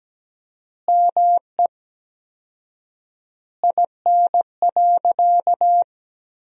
Morse code, heard as a keyed tone, about 17 wpm, 700 Hz. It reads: ME IN.
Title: 4x Word-Spacing - Sentences from Top 500 Words - 17wpm